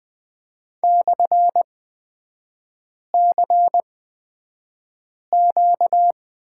Morse code, heard as a keyed tone, about 20 wpm, spaced as for 5 wpm, 690 Hz. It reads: /CQ
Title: 236 Lesson 38e - Letter q - Three Characters 20wpm